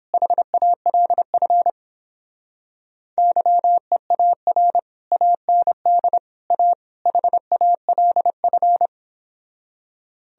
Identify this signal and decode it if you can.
Morse code, heard as a keyed tone, about 31 wpm, 700 Hz. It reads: HALF YEAR AND A HALF